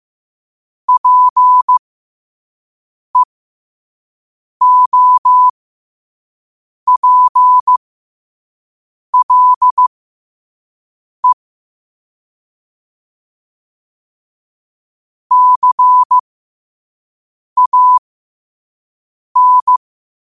Morse code, heard as a keyed tone, about 15 wpm, spaced as for 5 wpm, 1000 Hz. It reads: PEOPLE CAN
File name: qotdfiles / 5wpm